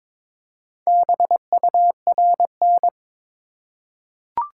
Morse code, heard as a keyed tone, about 22 wpm, 700 Hz. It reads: BURN
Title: Four Letter Words - 22wpm